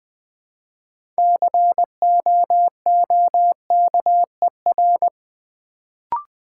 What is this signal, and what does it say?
Morse code, heard as a keyed tone, about 20 wpm, 700 Hz. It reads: COOKER
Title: Build A Word - 2 Letters At A Time - 6-Letter Words - 20wpm